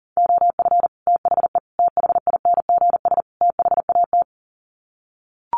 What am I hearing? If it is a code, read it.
Morse code, heard as a keyed tone, about 40 wpm, 700 Hz. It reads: OF THE THINGS THAT
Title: Sets of 4 Words 40wpm